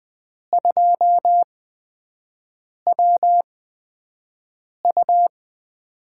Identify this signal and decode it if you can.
Morse code, heard as a keyed tone, about 20 wpm, spaced as for 5 wpm, 700 Hz. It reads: 2WU
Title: USA Call Signs - 8x Character Spacing - 20wpm